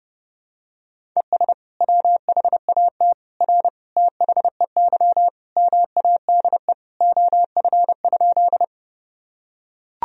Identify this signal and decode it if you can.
Morse code, heard as a keyed tone, about 30 wpm, 700 Hz. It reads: ES WHAT R THEY MADE OF?
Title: Sentences from Top 200 Words and Abbreviations 30wpm